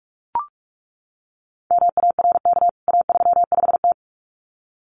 Morse code, heard as a keyed tone, about 44 wpm, 700 Hz. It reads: MARK A35T
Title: CWT Exchange 45wpm